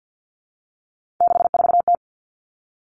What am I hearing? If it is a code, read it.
Morse code, heard as a keyed tone, about 50 wpm, 700 Hz. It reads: 64T
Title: Common Cut Numbers - 3-Digit 50wpm